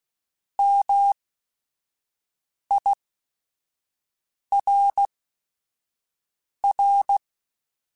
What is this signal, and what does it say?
Morse code, heard as a keyed tone, about 16 wpm, spaced as for 5 wpm, 800 Hz. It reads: MIRR